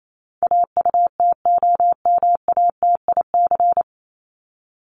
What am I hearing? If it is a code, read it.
Morse code, heard as a keyed tone, about 28 wpm, 700 Hz. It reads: AUTOMATIC
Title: Top 1751-2000 Words 28wpm